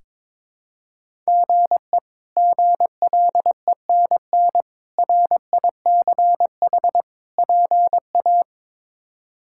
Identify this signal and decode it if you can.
Morse code, heard as a keyed tone, about 22 wpm, 700 Hz. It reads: GE GLENN RICH PA
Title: K1USN SST QSO Practice - 22wpm